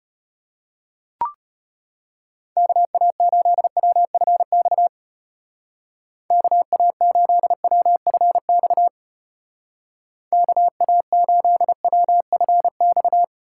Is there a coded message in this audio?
Yes: Morse code from 2.6 to 13.2 s, heard as a keyed tone, about 32 wpm, 700 Hz. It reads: KA8WFX KA8WFX KA8WFX